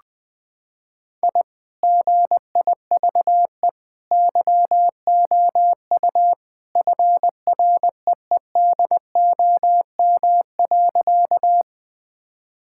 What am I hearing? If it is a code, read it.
Morse code, heard as a keyed tone, about 20 wpm, 700 Hz. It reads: I GIVE YOU FREEDOM.
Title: Sentences from Top 1000 Words - Review Part 5 - 20wpm